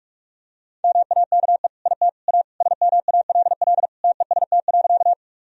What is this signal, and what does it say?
Morse code, heard as a keyed tone, about 46 wpm, 700 Hz. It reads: MAKE IT A SMALL TEST.
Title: Sentences from Top 1000 Words - Review Part 3 - 45wpm